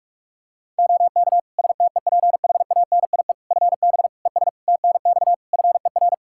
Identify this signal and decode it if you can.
Morse code, heard as a keyed tone, about 46 wpm, 700 Hz. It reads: OK STEPHANIE FB ES TNX FER
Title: Practice QSO - Exchange 2 - Info - 45wpm